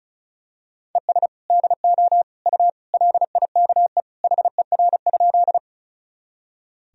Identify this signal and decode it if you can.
Morse code, heard as a keyed tone, about 36 wpm, 700 Hz. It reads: ES DO U LIKE HER?
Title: Sentences from Top 100 Words and Abbreviations 35wpm